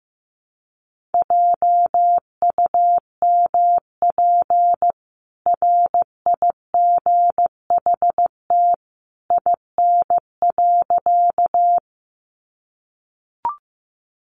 Morse code, heard as a keyed tone, about 14 wpm, 700 Hz. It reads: JUMP RIGHT IN.
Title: Sentences from Top 1300 Words 15wpm